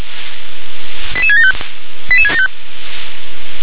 Five-tone selective call (ZVEI), heard as quick runs of five tones, tones of about 70 ms. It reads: 88675 88675
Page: /357/SDR/ZVEI/88771/